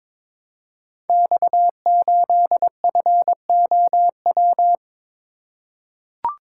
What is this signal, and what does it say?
Morse code, heard as a keyed tone, about 21 wpm, 700 Hz. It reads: X8FOW
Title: Code Groups - 5 Letter:Numbers - 22wpm